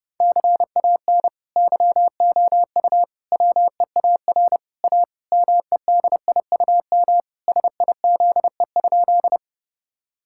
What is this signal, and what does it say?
Morse code, heard as a keyed tone, about 30 wpm, 700 Hz. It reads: CAN YOU WEAR A MEDIUM SIZE?
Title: Sentences from Top 1000 Words - Review Part 6 - 30wpm